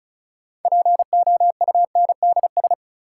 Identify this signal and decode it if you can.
Morse code, heard as a keyed tone, about 36 wpm, 700 Hz. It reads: POUNDS
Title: Sentences from Top 1000 Words - Review Part 2 - New Word Drill - 35wpm